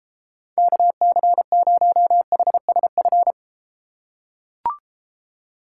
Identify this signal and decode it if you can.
Morse code, heard as a keyed tone, about 33 wpm, 700 Hz. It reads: KC0HSF